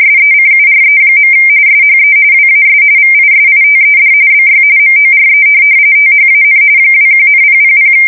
RTTY, about 45.45 baud. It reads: SECURE. SHE WAS VISITED BY MORE THAN 21,000
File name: fsk170 / USS-New-Jersey